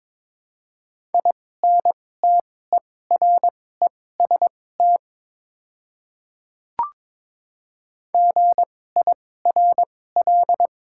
Morse code, heard as a keyed tone, about 22 wpm, spaced as for 16 wpm, 700 Hz. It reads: INTEREST GIRL
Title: 2x Character Spacing - Top 500 Words - 22wpm